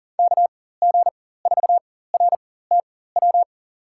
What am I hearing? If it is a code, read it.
Morse code, heard as a keyed tone, about 40 wpm, spaced as for 19 wpm, 700 Hz. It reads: KG4RTW
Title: USA Call Signs - 6 Characters - 4x Character Spacing - 40wpm